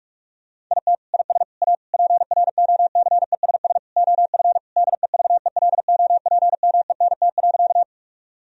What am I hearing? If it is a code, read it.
Morse code, heard as a keyed tone, about 41 wpm, 700 Hz. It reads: IT IS A PROCESS OF DEVELOPMENT.
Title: Sentences from Top 1000 Words - Review Part 4 - 45wpm